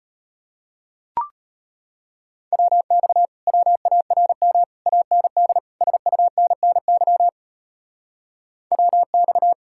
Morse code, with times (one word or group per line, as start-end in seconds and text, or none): 2.5-3.3 WX
3.5-4.6 WARM
4.9-5.6 AND
5.8-7.3 SUNNY
8.7-9.5 WX